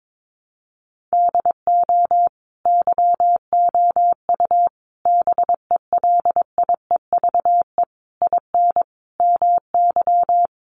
Morse code, heard as a keyed tone, about 22 wpm, 700 Hz. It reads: DO YOU BELIEVE IN MY
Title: Sentences from Top 1200 Words 22wpm